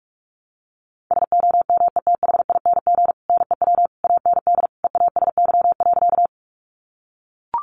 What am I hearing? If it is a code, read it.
Morse code, heard as a keyed tone, about 45 wpm, 700 Hz. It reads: SOMETHING NEW AND EASY.
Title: Sentences from Top 700 Words 45wpm